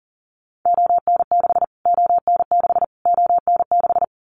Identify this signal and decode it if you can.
Morse code, heard as a keyed tone, about 40 wpm, 700 Hz. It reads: ON6 ON6 ON6
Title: Callsign Prefixes for Top 25 DXCC - Repeated 3x - 40wpm